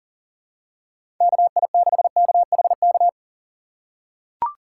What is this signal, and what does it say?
Morse code, heard as a keyed tone, about 40 wpm, 700 Hz. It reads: KI6KHK
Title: USA Call Signs - 3x Character Spacing - 40wpm